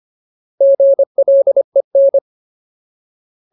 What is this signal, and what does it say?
Morse code, heard as a keyed tone, about 25 wpm, 550 Hz. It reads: GLEN
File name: Ultimate Word List / 25WPM Ultimate